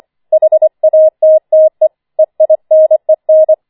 Morse code, heard as a keyed tone, about 25 wpm, 630 Hz. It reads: HATTE EINEN
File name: 25wpm